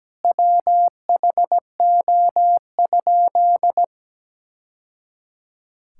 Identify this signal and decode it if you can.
Morse code, heard as a keyed tone, about 17 wpm, 700 Hz. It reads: WHO?
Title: Sentences from Top 1000 Words - Review Part 5 - 17wpm